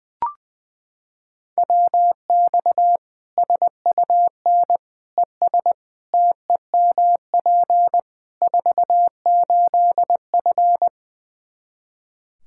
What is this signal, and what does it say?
Morse code, heard as a keyed tone, about 21 wpm, 700 Hz. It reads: WX SUN ES TEMP 48F
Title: K1USN SST QSO Practice 20wpm